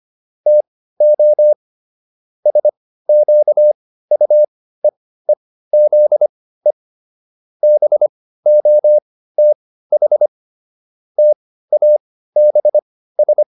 Morse code, heard as a keyed tone, about 25 wpm, spaced as for 15 wpm, 600 Hz. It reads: TO SQUEEZE BOTH TABS